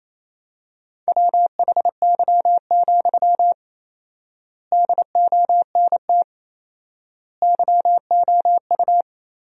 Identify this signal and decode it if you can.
Morse code, heard as a keyed tone, about 28 wpm, 700 Hz. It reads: WHY, DONT YOU
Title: 4x Word-Spacing - Sentences from Top 300 Words - 28wpm